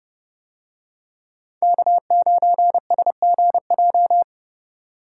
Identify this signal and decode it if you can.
Morse code, heard as a keyed tone, about 28 wpm, 700 Hz. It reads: K9SGJ